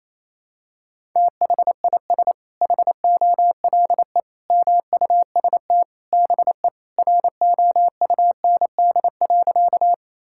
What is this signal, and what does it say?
Morse code, heard as a keyed tone, about 28 wpm, 700 Hz. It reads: THIS HOLE MUST BE ROUND.
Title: Sentences from Top 1600 Words 28wpm